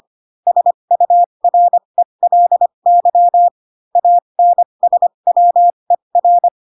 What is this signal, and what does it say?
Morse code, heard as a keyed tone, about 25 wpm, spaced as for 22 wpm, 700 Hz. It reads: SURELY ANSWER
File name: Pinocchio / Pinocchio ch6 18 25 ik1hge